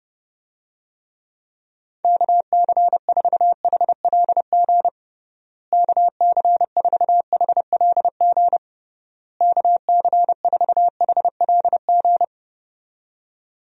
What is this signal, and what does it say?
Morse code, heard as a keyed tone, about 30 wpm, 700 Hz. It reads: KC4HLG KC4HLG KC4HLG
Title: USA Call Signs - 6 Characters - Repeated 3x - 30wpm